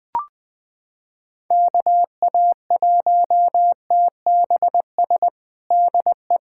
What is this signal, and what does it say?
Morse code, heard as a keyed tone, about 20 wpm, 700 Hz. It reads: KA1TBS DE
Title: USA Callsign DE Callsign 20wpm